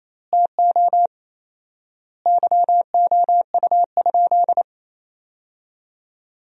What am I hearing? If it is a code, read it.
Morse code, heard as a keyed tone, about 28 wpm, 700 Hz. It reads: TO YOU?
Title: 4x Word-Spacing - Sentences from Top 300 Words - 28wpm